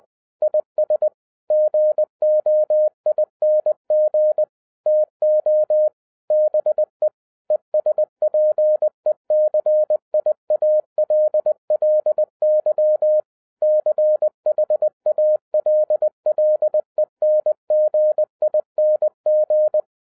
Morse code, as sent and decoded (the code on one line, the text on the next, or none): .. ... / --. --- .. -. --. / - --- / -... / . ... .--. -.-. .. .- .-.. .-.. -.-- / -.-. .... .- .-.. .-.. -. --. .. -. --.
IS GOING TO BE ESPECIALLY CHALLENGING